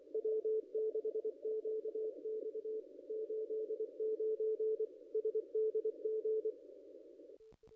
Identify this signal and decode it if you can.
Morse code, heard as a keyed tone, about 24 wpm, 440 Hz, with radio noise. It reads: W6QX 89 SDG
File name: contests / ss-cw-k5tr-op-n5rz -0200